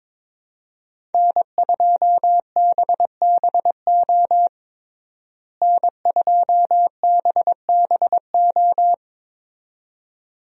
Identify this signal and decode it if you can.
Morse code, heard as a keyed tone, about 22 wpm, 700 Hz. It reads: N2BBO N2BBO
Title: USA Call Signs - 5 Characters - Repeated 2x - 22wpm